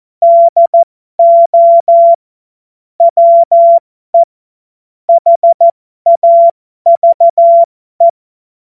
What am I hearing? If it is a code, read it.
Morse code, heard as a keyed tone, about 14 wpm, spaced as for 12 wpm, 680 Hz. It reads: DO WE HAVE
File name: Morsekurs / Witze